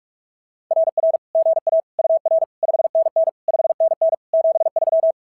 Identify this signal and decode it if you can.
Morse code, heard as a keyed tone, about 46 wpm, 650 Hz. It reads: RR GA UR 5NN 5NN 73
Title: SOTA QSO Practice 45wpm